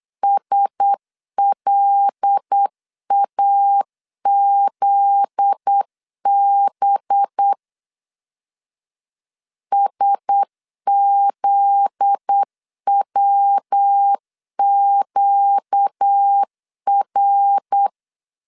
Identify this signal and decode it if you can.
Morse code, heard as a keyed tone, about 8 wpm, 800 Hz. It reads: SLAZB SZWQR
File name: LEZIONI / 7 GF9Z / 7 GF9Z 8wpm 3213